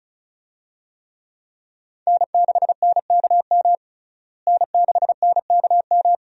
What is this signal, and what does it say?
Morse code, heard as a keyed tone, about 34 wpm, 700 Hz. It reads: N6NKM N6NKM
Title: USA Call Signs - 5 Characters - Repeated 2x - 35wpm